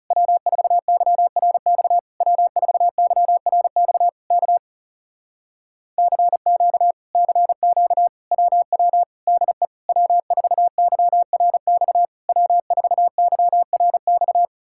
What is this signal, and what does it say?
Morse code, heard as a keyed tone, about 40 wpm, 700 Hz. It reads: W4YRX W4YRX K CQ CQ WW DE W4YRX W4YRX